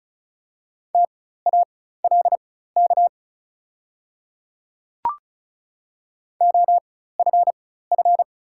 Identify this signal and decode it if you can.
Morse code, heard as a keyed tone, about 36 wpm, spaced as for 16 wpm, 690 Hz. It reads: TALK OFF
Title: 4x Character Spacing - Top 500 Words - 35wpm